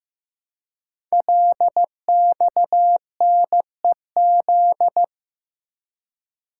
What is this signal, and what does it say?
Morse code, heard as a keyed tone, about 15 wpm, 700 Hz. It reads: LXNEZ
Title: Code Groups - 5 Letters - 15wpm